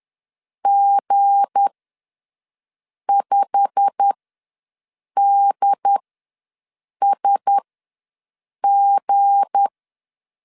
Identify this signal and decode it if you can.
Morse code, heard as a keyed tone, about 10 wpm, spaced as for 6 wpm, 800 Hz. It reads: G5DSG
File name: LEZIONI / 8 YJ45 / 8 YJ45 ws 10wpm 3843